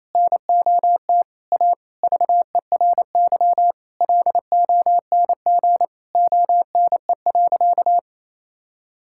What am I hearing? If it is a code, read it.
Morse code, heard as a keyed tone, about 28 wpm, 700 Hz. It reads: NOT A VERY LONG ONE.
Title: Sentences from Top 1000 Words - Review Part 5 - 28wpm